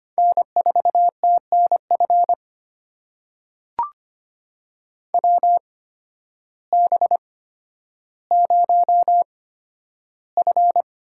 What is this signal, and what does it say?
Morse code, heard as a keyed tone, about 25 wpm, spaced as for 12 wpm, 700 Hz. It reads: N4TNF WB0F
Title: USA Call Signs - 8x Character Spacing - 25wpm